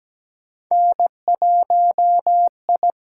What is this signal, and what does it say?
Morse code, heard as a keyed tone, about 17 wpm, 700 Hz. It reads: N1I
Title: USA Call Signs - 5 Characters - 8x Character Spacing - 17wpm